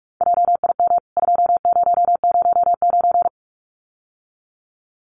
Morse code, heard as a keyed tone, about 44 wpm, 700 Hz. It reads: JIM 2009
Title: CWT Exchange 45wpm